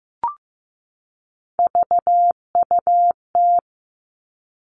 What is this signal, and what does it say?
Morse code, heard as a keyed tone, about 15 wpm, 700 Hz. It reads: VUT
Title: All Cut Numbers - 3-Digit 15wpm